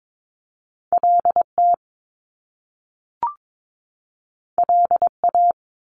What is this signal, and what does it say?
Morse code, heard as a keyed tone, about 22 wpm, 700 Hz. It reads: LT LA
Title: Top Two Letter Combinations in Words 22wpm